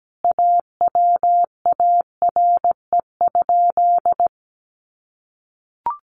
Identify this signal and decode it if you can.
Morse code, heard as a keyed tone, about 17 wpm, 700 Hz. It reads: AWARE?
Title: Sentences from Top 1200 Words 17wpm